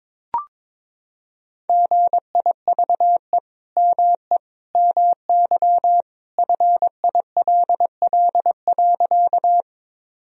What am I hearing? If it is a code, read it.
Morse code, heard as a keyed tone, about 22 wpm, 700 Hz. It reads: GIVE ME MY FILL.
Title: Sentences from Top 1000 Words - Review Part 3 - 22wpm